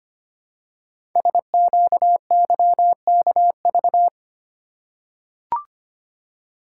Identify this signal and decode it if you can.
Morse code, heard as a keyed tone, about 25 wpm, 700 Hz. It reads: SQYKV
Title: Code Groups - 5 Letters - 25wpm